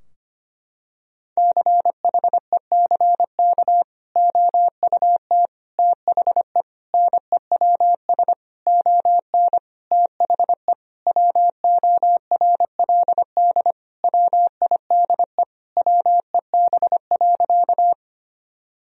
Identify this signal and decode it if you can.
Morse code, heard as a keyed tone, about 25 wpm, 700 Hz. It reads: CHECK OUT THE NEWS ON THE WORLD WIDE WEB.
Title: Sentences from Top 1000 Words - Review Part 4 - 25wpm